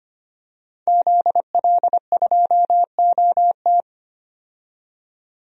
Morse code, heard as a keyed tone, about 25 wpm, 700 Hz. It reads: ZL2OT
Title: DXCC-100 Call Signs - 5 Characters - 25wpm